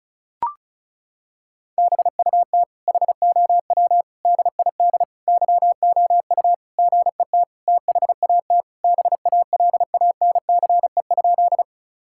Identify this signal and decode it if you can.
Morse code, heard as a keyed tone, about 36 wpm, 700 Hz. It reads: BUT HOW DID YOU GET THAT BALANCE?
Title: Sentences from Top 1000 Words - Review Part 6 - 35wpm